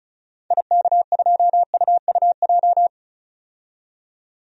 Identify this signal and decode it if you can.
Morse code, heard as a keyed tone, about 36 wpm, 700 Hz. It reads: IK2UUJ
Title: DXCC-100 Call Signs - 6 Characters - 35wpm